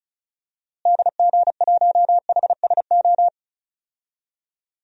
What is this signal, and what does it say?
Morse code, heard as a keyed tone, about 34 wpm, 700 Hz. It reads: DG1HSO
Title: DXCC-100 Call Signs - 6 Characters - Repeated 2x - 35wpm